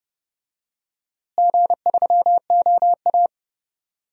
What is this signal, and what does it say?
Morse code, heard as a keyed tone, about 30 wpm, 700 Hz. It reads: G3OA
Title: DXCC-100 Call Signs - 4 Characters - Repeated 2x - 30wpm